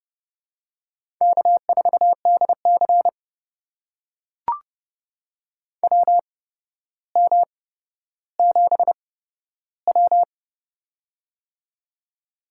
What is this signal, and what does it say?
Morse code, heard as a keyed tone, about 30 wpm, spaced as for 13 wpm, 700 Hz. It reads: K4DC WM7W
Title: USA Call Signs - 4 Characters - 8x Character Spacing - 30wpm